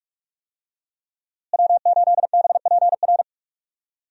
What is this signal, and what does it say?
Morse code, heard as a keyed tone, about 46 wpm, 700 Hz. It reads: W8BPR
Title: USA Call Signs - 5 Characters - Repeated 3x - 45wpm